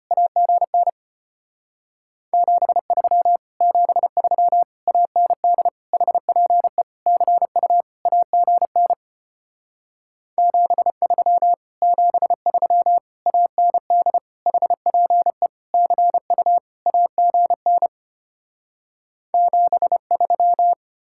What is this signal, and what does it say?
Morse code, heard as a keyed tone, about 33 wpm, 700 Hz. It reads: AGN 73 73 AND HPE CU AGN 73 73 AND HPE CU AGN 73